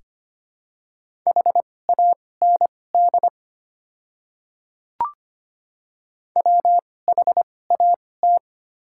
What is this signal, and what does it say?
Morse code, heard as a keyed tone, about 25 wpm, spaced as for 18 wpm, 700 Hz. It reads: HAND WHAT
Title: 2x Character Spacing - Top 300 Words - 25wpm